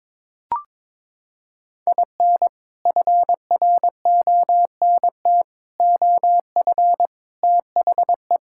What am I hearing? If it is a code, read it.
Morse code, heard as a keyed tone, about 21 wpm, 700 Hz. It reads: IN FRONT OF THE
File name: practice / 22wpm / Sentences with 4-Word Phrases - 22wpm